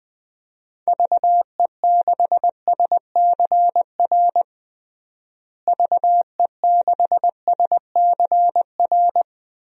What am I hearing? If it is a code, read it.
Morse code, heard as a keyed tone, about 20 wpm, 700 Hz. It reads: VE6SCR VE6SCR
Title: DXCC-100 Call Signs - 6 Characters - Repeated 3x - 20wpm